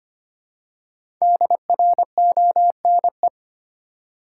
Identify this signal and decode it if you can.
Morse code, heard as a keyed tone, about 25 wpm, 700 Hz. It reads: DRONE
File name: practice / 25wpm / Difficult Words with Hints - 25wpm